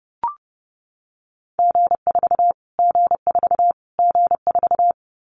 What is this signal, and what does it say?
Morse code, heard as a keyed tone, about 30 wpm, 700 Hz. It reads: G4 G4 G4
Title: Callsign Prefixes for Top 25 DXCC - Repeated 3x - 30wpm